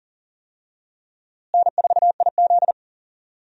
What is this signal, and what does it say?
Morse code, heard as a keyed tone, about 39 wpm, 700 Hz. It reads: N4IZ
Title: USA Call Signs - 4 Characters - Repeated 2x - 40wpm